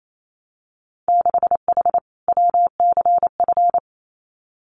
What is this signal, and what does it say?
Morse code, heard as a keyed tone, about 28 wpm, 700 Hz. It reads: 6H WCF
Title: Winter Field Day Exchanges Encore 28wpm